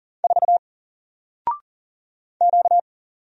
Morse code, heard as a keyed tone, about 41 wpm, 700 Hz. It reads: 4 Q
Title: Single Character (Letter:Number:Punctuation:Prosign) Alt 40wpm